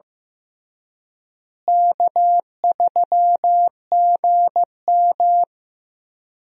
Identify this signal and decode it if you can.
Morse code, heard as a keyed tone, about 16 wpm, 700 Hz. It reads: K3GM
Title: USA Call Signs - 4 Characters - Repeated 3x - 15wpm